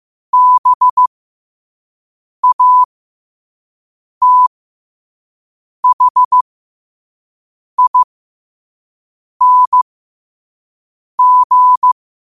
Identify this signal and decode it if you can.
Morse code, heard as a keyed tone, about 15 wpm, spaced as for 5 wpm, 1000 Hz. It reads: BATHING